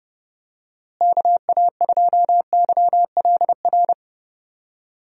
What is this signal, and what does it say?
Morse code, heard as a keyed tone, about 30 wpm, 700 Hz. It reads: KA2YLR
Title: USA Call Signs - 6 Characters - Repeated 3x - 30wpm